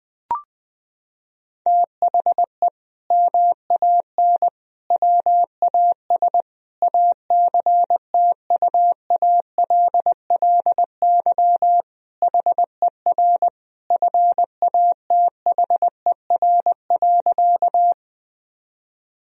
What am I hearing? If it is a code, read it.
Morse code, heard as a keyed tone, about 20 wpm, 700 Hz. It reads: THE MAN WAS ACTUALLY HER FATHER.
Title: Sentences from Top 1000 Words - Review Part 4 - 20wpm